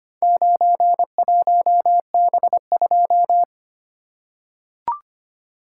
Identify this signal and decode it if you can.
Morse code, heard as a keyed tone, about 25 wpm, 700 Hz. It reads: 91B2
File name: practice / 25wpm / Four Letter-Number Combinations - 25wpm